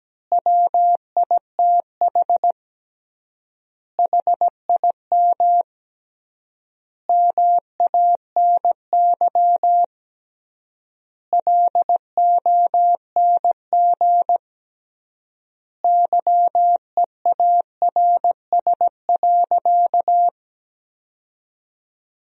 Morse code, heard as a keyed tone, about 17 wpm, 700 Hz. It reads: WITH HIM MANY LONG YEARS.
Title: 3x Word-Spacing - Sentences from Top 200 Words - 17wpm